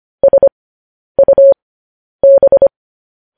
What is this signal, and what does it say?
Morse code, heard as a keyed tone, about 25 wpm, spaced as for 10 wpm, 550 Hz. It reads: SUB
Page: news headlines in morse code at 10 wpm